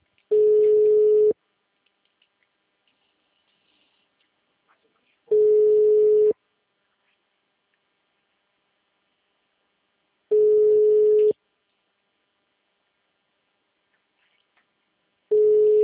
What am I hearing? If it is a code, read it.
Ringback tone (Europe, 425 Hz).
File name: out- -2008- - - .7607